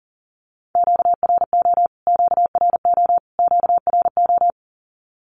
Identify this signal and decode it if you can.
Morse code, heard as a keyed tone, about 40 wpm, 700 Hz. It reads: QRO QRO QRO
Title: Q-Codes Repeated 3x 40wpm